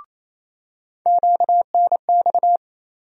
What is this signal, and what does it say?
Morse code, heard as a keyed tone, about 28 wpm, 700 Hz. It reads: QNX
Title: USA Call Sign Postfixes - 3 Letters 28wpm